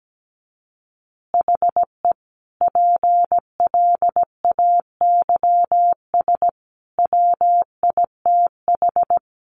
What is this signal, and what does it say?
Morse code, heard as a keyed tone, about 17 wpm, 700 Hz. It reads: HE PLAYS WITH
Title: Easy Phrases 17wpm